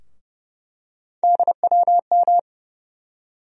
Morse code, heard as a keyed tone, about 30 wpm, 700 Hz. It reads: DWM
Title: USA Call Sign Postfixes - 3 Letters 30wpm